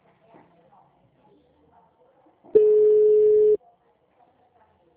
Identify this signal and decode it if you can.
Ringback tone (Europe, 425 Hz), with speech.